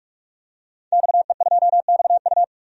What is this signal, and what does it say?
Morse code, heard as a keyed tone, about 42 wpm, 700 Hz. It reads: XE2XU
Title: DXCC-100 Call Signs - 5 Characters - 45wpm